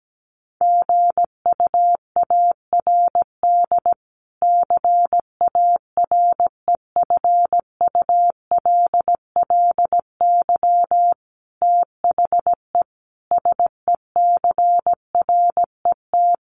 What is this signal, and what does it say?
Morse code, heard as a keyed tone, about 17 wpm, 700 Hz. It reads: GUARD CAREFULLY THE SECRET
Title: Sentences from Top 1300 Words 17wpm